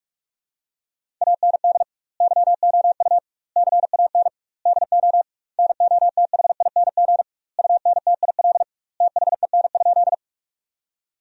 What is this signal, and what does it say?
Morse code, heard as a keyed tone, about 43 wpm, 700 Hz. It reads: AND YOU CAN DO NOTHING UNTIL THEN?
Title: 2x Word-Spacing - Sentences from Top 500 Words - 45wpm